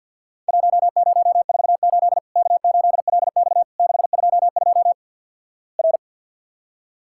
Morse code, heard as a keyed tone, about 47 wpm, 700 Hz. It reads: 1048 K8LX 622 R
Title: Worked All Europe DX Contest - QTC Exchange - 50wpm